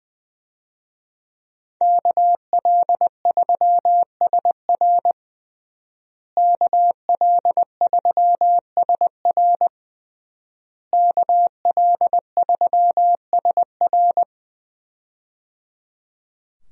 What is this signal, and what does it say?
Morse code, heard as a keyed tone, about 20 wpm, 700 Hz. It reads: KL3SR KL3SR KL3SR
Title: USA Call Signs - 5 Characters - Repeated 3x - 20wpm